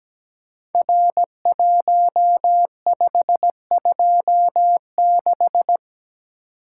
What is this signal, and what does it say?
Morse code, heard as a keyed tone, about 17 wpm, 700 Hz. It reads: R1526
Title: Code Groups - 5 Letter:Numbers - 17wpm